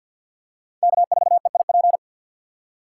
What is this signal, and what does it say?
Morse code, heard as a keyed tone, about 46 wpm, 700 Hz. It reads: K4EIP
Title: USA Call Signs - 5 Characters - 6x Character Spacing - 50wpm